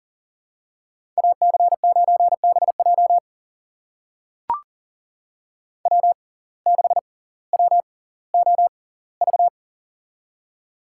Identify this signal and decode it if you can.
Morse code, heard as a keyed tone, about 40 wpm, spaced as for 20 wpm, 700 Hz. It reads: AC9BJ W6WOV
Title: USA Call Signs - 5 Characters - 6x Character Spacing - 40wpm